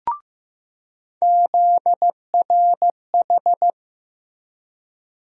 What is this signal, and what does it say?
Morse code, heard as a keyed tone, about 16 wpm, 700 Hz. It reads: ZRH
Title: USA Call Sign Postfixes - 3 Letters 15wpm